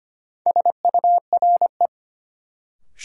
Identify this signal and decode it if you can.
Morse code, heard as a keyed tone, about 25 wpm, 700 Hz. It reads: SURE
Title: Sets of 3 Words - New Word Drill - 25wpm